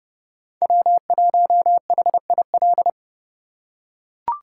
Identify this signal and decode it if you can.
Morse code, heard as a keyed tone, about 30 wpm, 700 Hz. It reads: W1HIL
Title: USA Call Signs - 5 Characters - 2x Character Spacing - 30wpm